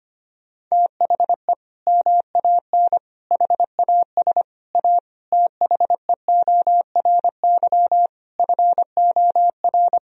Morse code, heard as a keyed tone, about 25 wpm, 700 Hz. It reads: THE MAN HAS A THEORY FOR
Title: Sentences from Top 1000 Words - Review Part 3 - 25wpm